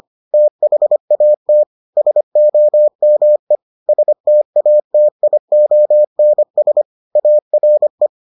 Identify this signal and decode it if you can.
Morse code, heard as a keyed tone, about 25 wpm, 600 Hz. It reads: THAT SOME STATIONS ARE